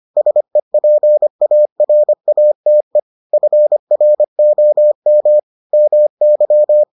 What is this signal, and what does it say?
Morse code, heard as a keyed tone, about 25 wpm, 600 Hz. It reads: SEPARATE FROM MY